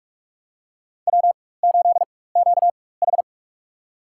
Morse code, heard as a keyed tone, about 46 wpm, spaced as for 21 wpm, 700 Hz. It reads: W8QH
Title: USA Call Signs - 4 Characters - 4x Character Spacing - 45wpm